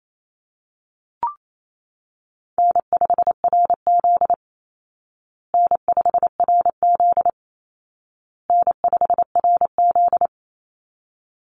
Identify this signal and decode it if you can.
Morse code, heard as a keyed tone, about 28 wpm, 700 Hz. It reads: N5RZ N5RZ N5RZ
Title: Easy Call Signs Repeated 3x - 28wpm